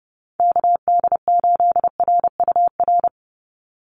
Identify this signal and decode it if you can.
Morse code, heard as a keyed tone, about 29 wpm, 700 Hz. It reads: KD8RUR
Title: USA Call Signs 30wpm